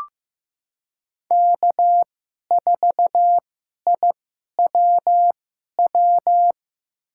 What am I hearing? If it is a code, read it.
Morse code, heard as a keyed tone, about 16 wpm, spaced as for 11 wpm, 700 Hz. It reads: K4IWW
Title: USA Call Signs - 5 Characters - 2x Character Spacing - 15wpm